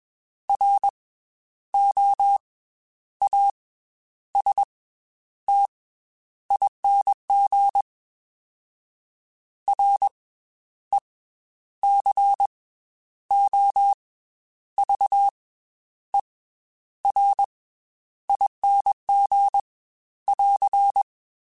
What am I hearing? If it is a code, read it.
Morse code, heard as a keyed tone, about 21 wpm, 800 Hz. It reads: R O A S T ING R E C O V E R ING <AR>